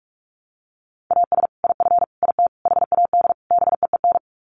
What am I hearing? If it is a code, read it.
Morse code, heard as a keyed tone, about 44 wpm, 700 Hz. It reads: AS IF IT HAD BEEN
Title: Sets of 5 Words Encore2 45wpm